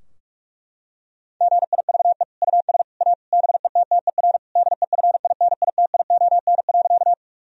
Morse code, heard as a keyed tone, about 41 wpm, 700 Hz. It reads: GIVE US A BETTER DEFINITION.
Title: Sentences from Top 1000 Words - Review Part 4 - 45wpm